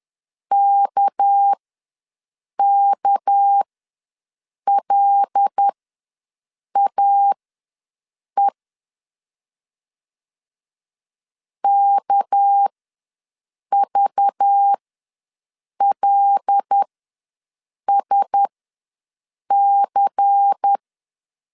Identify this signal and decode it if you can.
Morse code, heard as a keyed tone, about 10 wpm, spaced as for 6 wpm, 800 Hz. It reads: KKLAE KVLSC
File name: LEZIONI / 6 PKX0 / 6 PK ws 10wpm 3745